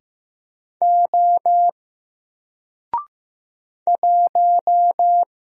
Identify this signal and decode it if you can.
Morse code, heard as a keyed tone, about 15 wpm, 700 Hz. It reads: O 1
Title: Single Letter-Number 15wpm